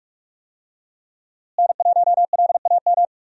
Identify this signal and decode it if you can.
Morse code, heard as a keyed tone, about 46 wpm, 700 Hz. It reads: N1LAM
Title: USA Call Signs - 5 Characters - Repeated 3x - 45wpm